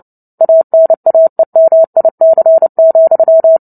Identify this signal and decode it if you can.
Morse code, heard as a keyed tone, about 30 wpm, 650 Hz. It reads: ANAEMIC,
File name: Book Two / 03-Chapter 3 - 29 WPM